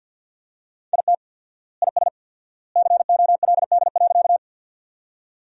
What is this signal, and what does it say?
Morse code, heard as a keyed tone, about 50 wpm, 700 Hz. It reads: IT IS COLD.
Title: 4x Word-Spacing - Sentences from Top 400 Words - 50wpm